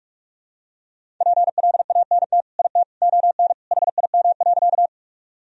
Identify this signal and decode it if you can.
Morse code, heard as a keyed tone, about 41 wpm, 700 Hz. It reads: PLANT IT ON HIM.
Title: Sentences from Top 1000 Words - Review Part 1 - 45wpm